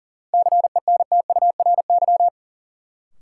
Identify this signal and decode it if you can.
Morse code, heard as a keyed tone, about 40 wpm, 700 Hz. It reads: CENTURY
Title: Sets of 5 Words - New Word Drill - 40wpm